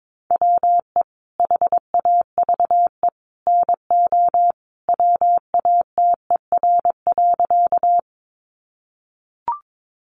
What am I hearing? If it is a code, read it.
Morse code, heard as a keyed tone, about 22 wpm, 700 Hz. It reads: WE HAVE NO WATER.
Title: Sentences from Top 100 Words 22wpm